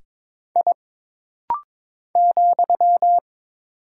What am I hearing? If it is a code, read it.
Morse code, heard as a keyed tone, about 22 wpm, 700 Hz. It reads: I ,
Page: This rapid-fire timing is geared to push your ICR (Instant Character Recognition) skills.